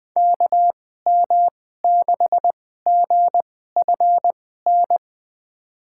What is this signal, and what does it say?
Morse code, heard as a keyed tone, about 20 wpm, spaced as for 14 wpm, 700 Hz. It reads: KM6GFN
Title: USA Call Signs - 6 Characters - 2x Character Spacing - 20wpm